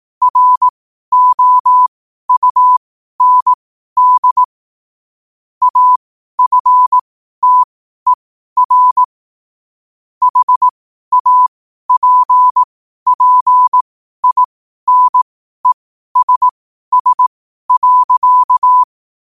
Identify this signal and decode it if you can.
Morse code, heard as a keyed tone, about 18 wpm, spaced as for 12 wpm, 1000 Hz. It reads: ROUND AFTER HAPPINESS.
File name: qotdfiles / 12wpm